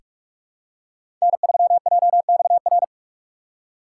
Morse code, heard as a keyed tone, about 44 wpm, 700 Hz. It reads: N3JXR
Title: USA Call Signs - 5 Characters - 6x Character Spacing - 45wpm